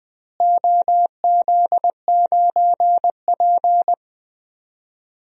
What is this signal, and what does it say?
Morse code, heard as a keyed tone, about 20 wpm, 700 Hz. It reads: OZ9P
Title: DXCC-100 Call Signs - 4 Characters - Repeated 3x - 20wpm